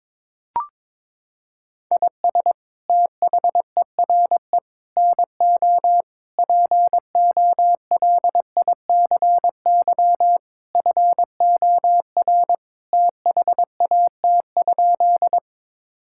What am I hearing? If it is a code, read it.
Morse code, heard as a keyed tone, about 22 wpm, 700 Hz. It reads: IS THERE NO POLICY FOR THAT?
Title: Sentences from Top 1000 Words - Review Part 5 - 22wpm